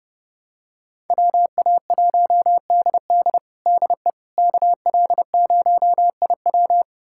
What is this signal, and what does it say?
Morse code, heard as a keyed tone, about 30 wpm, 700 Hz. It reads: WA1DD DE KL0IW
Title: USA Callsign DE Callsign - 5 Characters - 30wpm